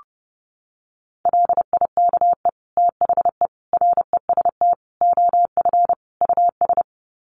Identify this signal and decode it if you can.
Morse code, heard as a keyed tone, about 30 wpm, 700 Hz. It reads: LIKE THE REST OF US
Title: Sets of 5 Words Encore2 30wpm